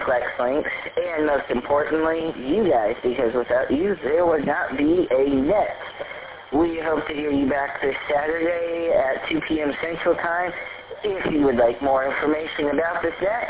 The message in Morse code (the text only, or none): none